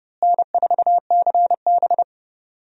Morse code, heard as a keyed tone, about 30 wpm, 700 Hz. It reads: N4CB
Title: USA Call Signs - 4 Characters - Repeated 3x - 30wpm